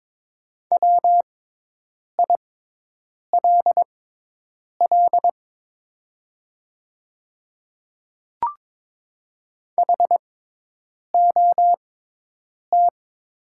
Morse code, heard as a keyed tone, about 22 wpm, spaced as for 8 wpm, 700 Hz. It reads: WILL HOT
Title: 6x Character Spacing - Top 200 Words - 22wpm